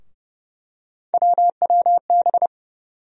Morse code, heard as a keyed tone, about 31 wpm, 700 Hz. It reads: WWB
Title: USA Call Sign Postfixes - 3 Letters 30wpm